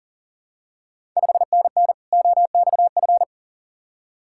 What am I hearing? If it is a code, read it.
Morse code, heard as a keyed tone, about 40 wpm, 690 Hz. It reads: 5NN OXF
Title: Ontario QSO Party Exchange 40wpm